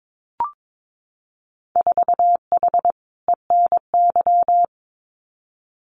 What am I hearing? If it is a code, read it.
Morse code, heard as a keyed tone, about 22 wpm, 700 Hz. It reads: 4H ENY
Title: Winter Field Day Exchanges Encore 22wpm